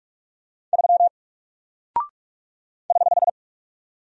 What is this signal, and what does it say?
Morse code, heard as a keyed tone, about 45 wpm, 700 Hz. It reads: 3 <HH>
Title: Single Character (Letter:Number:Punctuation:Prosign) Alt 45wpm